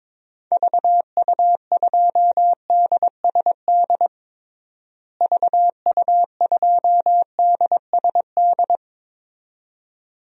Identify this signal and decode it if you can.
Morse code, heard as a keyed tone, about 22 wpm, 700 Hz. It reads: VU2DSD VU2DSD
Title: DXCC-100 Call Signs - 6 Characters - Repeated 2x - 22wpm